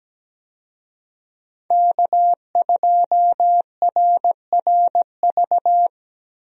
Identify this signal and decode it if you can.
Morse code, heard as a keyed tone, about 17 wpm, 700 Hz. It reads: K2RRV
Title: USA Call Signs - 5 Characters - Repeated 3x - 17wpm